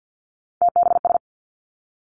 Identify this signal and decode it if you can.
Morse code, heard as a keyed tone, about 50 wpm, 700 Hz. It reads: TBS
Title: Three Challenging Letter-Number Combinations 50wpm